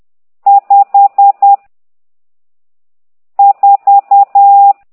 Morse code, heard as a keyed tone, about 10 wpm, 800 Hz. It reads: 5 4
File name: ct2idw / exerc12